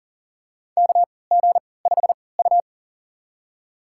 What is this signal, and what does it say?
Morse code, heard as a keyed tone, about 40 wpm, spaced as for 23 wpm, 700 Hz. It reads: KG5U
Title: USA Call Signs - 3x Character Spacing - 40wpm